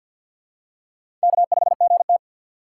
Morse code, heard as a keyed tone, about 52 wpm, 700 Hz. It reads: K5GT
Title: USA Call Signs - 4 Characters - 3x Character Spacing - 50wpm